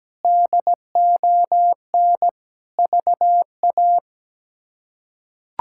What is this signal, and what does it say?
Morse code, heard as a keyed tone, about 17 wpm, 700 Hz. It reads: DON VA
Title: K1USN SST QSO Practice - 17wpm